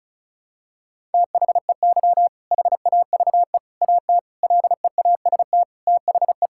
Morse code, heard as a keyed tone, about 36 wpm, 700 Hz. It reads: THEY HAVE AT LEAST THE
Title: Sentences from Top 1000 Words - Review Part 6 - 35wpm